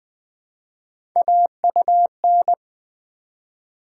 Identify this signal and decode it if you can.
Morse code, heard as a keyed tone, about 20 wpm, 700 Hz. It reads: AUN
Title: Build A Word - 1 Letter At A Time - 3 to 5-Letter Words - 20wpm